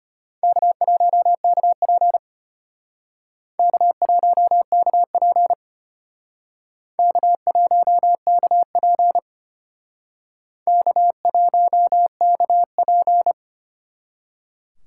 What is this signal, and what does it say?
Morse code, heard as a keyed tone, about 32 wpm, 700 Hz. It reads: K1KP K1KP K1KP K1KP